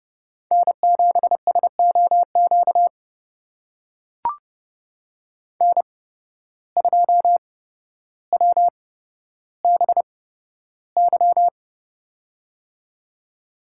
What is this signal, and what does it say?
Morse code, heard as a keyed tone, about 31 wpm, spaced as for 13 wpm, 700 Hz. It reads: N7SOQ N2WBY
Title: USA Call Signs - 5 Characters - 8x Character Spacing - 30wpm